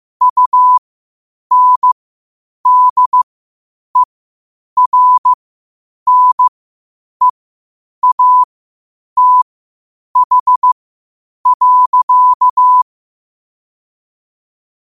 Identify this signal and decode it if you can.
Morse code, heard as a keyed tone, about 15 wpm, spaced as for 8 wpm, 1000 Hz. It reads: UNDERNEATH.
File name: qotdfiles / 8wpm